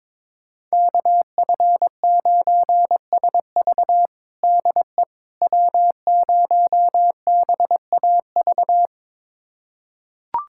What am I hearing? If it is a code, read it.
Morse code, heard as a keyed tone, about 22 wpm, 700 Hz. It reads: KF9SV DE W0BAV
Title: USA Callsign DE Callsign - 5 Characters - 22wpm